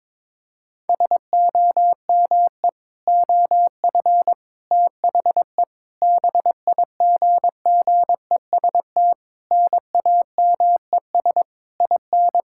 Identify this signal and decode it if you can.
Morse code, heard as a keyed tone, about 22 wpm, 700 Hz. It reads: SOME OF THE BIGGEST NAMES IN
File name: Sentences with 7-Word Phrases - 22wpm